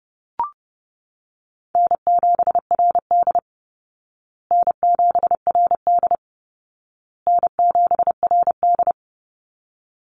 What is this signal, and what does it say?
Morse code, heard as a keyed tone, about 30 wpm, 700 Hz. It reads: N7RD N7RD N7RD
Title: Easy Call Signs Repeated 3x - 30wpm